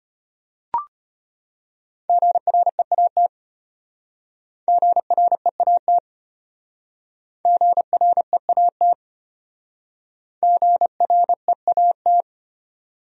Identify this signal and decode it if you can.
Morse code, heard as a keyed tone, about 32 wpm, 700 Hz. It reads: GREAT GREAT GREAT GREAT